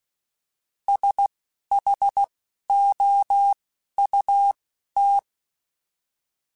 Morse code, heard as a keyed tone, about 16 wpm, 800 Hz. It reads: S H O U T
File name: S vs H / S vs H mp3s / S vs H @10